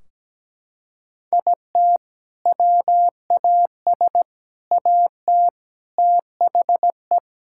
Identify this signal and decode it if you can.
Morse code, heard as a keyed tone, about 17 wpm, 700 Hz. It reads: IT WAS AT THE
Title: Sentences from Top 1600 Words 17wpm